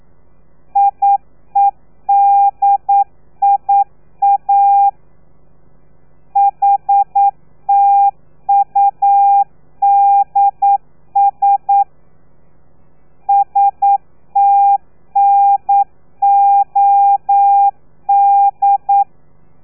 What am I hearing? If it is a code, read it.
Morse code, heard as a keyed tone, about 9 wpm, 800 Hz. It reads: IEDIA HTUDS STNOD
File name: telegrafia / i2tfj / Lez3 40